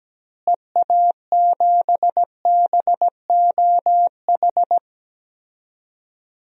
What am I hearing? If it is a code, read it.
Morse code, heard as a keyed tone, about 17 wpm, 700 Hz. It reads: EA7BOH
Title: DXCC-100 Call Signs - 6 Characters - Repeated 3x - 17wpm